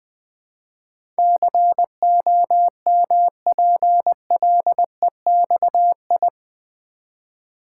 Morse code, heard as a keyed tone, about 21 wpm, 700 Hz. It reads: COMPLEXI
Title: Build A Word - 4 Letters At A Time - 10-Letter Words - 20wpm